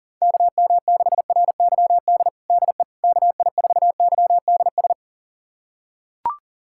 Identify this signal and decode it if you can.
Morse code, heard as a keyed tone, about 37 wpm, 700 Hz. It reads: KM6RYD DE KI4YDS
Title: USA Callsign DE Callsign - 6 Characters - 40wpm